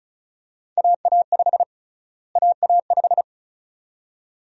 Morse code, heard as a keyed tone, about 36 wpm, 700 Hz. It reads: AA5 AA5
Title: USA Call Sign Prefixes - Repeated 2x - 35wpm